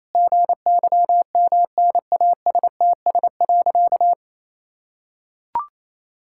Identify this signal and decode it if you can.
Morse code, heard as a keyed tone, about 28 wpm, 700 Hz. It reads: GYMNASTS.
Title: Pangram Challenge v2 - 28wpm